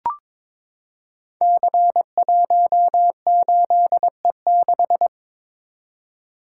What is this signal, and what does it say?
Morse code, heard as a keyed tone, about 22 wpm, 700 Hz. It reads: C18E6
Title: Code Groups - 5 Letter:Numbers - 22wpm